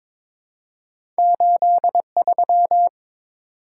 Morse code, heard as a keyed tone, about 22 wpm, 700 Hz. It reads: 83
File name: practice / 22wpm / Two Letter-Number Combinations - 22wpm